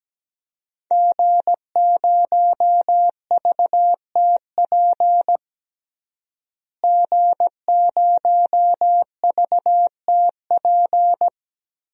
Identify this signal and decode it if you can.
Morse code, heard as a keyed tone, about 17 wpm, 700 Hz. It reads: G0VTP G0VTP
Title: DXCC-100 Call Signs - 5 Characters - Repeated 3x - 17wpm